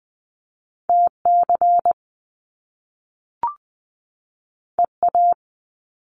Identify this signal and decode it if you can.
Morse code, heard as a keyed tone, about 20 wpm, 700 Hz. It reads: TC EA
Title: Top Two Letter Combinations in Words 20wpm